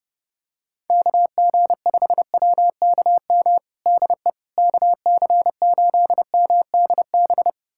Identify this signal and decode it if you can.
Morse code, heard as a keyed tone, about 29 wpm, 700 Hz. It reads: KG5WKM DE KC8MDB
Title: USA Callsign DE Callsign 30wpm